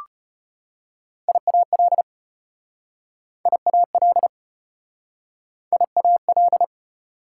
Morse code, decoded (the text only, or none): IAL IAL IAL